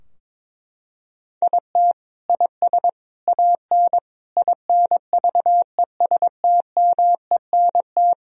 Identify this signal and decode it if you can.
Morse code, heard as a keyed tone, about 22 wpm, 700 Hz. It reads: IT IS AN INVESTMENT
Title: Sentences from Top 1000 Words - Review Part 4 - 22wpm